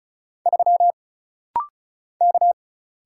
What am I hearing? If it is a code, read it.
Morse code, heard as a keyed tone, about 35 wpm, 700 Hz. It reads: 3 K
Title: Single Letter-Number - Rapid-Fire 35wpm